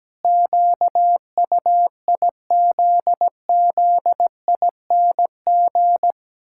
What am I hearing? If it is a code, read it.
Morse code, heard as a keyed tone, about 17 wpm, 700 Hz. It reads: QUIZZING
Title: Pangram Challenge v2 - 17wpm